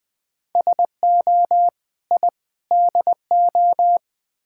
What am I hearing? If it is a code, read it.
Morse code, heard as a keyed tone, about 20 wpm, 690 Hz. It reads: SO I DO
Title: Build A Phrase - 4-Word Phrases - 20wpm